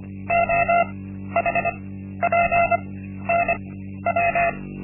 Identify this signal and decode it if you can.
Morse code, heard as a keyed tone, about 25 wpm, spaced as for 12 wpm, 690 Hz. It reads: OHPNW
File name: Linksmakalnis-1